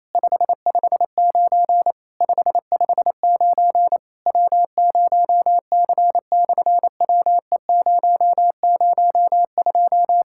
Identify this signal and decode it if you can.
Morse code, heard as a keyed tone, about 28 wpm, 700 Hz. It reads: 559 559 W0C/WE002
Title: SOTA QSO Practice 28wpm